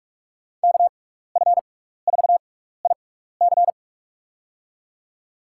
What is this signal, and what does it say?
Morse code, heard as a keyed tone, about 47 wpm, spaced as for 16 wpm, 700 Hz. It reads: KF4IC
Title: USA Call Signs - 5 Characters - 6x Character Spacing - 45wpm